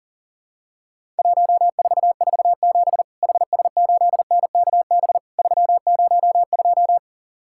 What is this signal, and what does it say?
Morse code, heard as a keyed tone, about 39 wpm, 700 Hz. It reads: 1447 HS8NKB 302
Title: Worked All Europe DX Contest - QTC Exchange - 40wpm